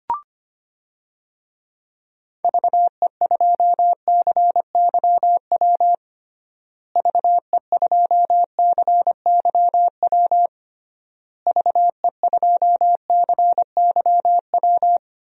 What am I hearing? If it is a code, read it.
Morse code, heard as a keyed tone, about 25 wpm, 700 Hz. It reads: VE2CYW VE2CYW VE2CYW
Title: DXCC-100 Call Signs - 6 Characters - Repeated 3x - 25wpm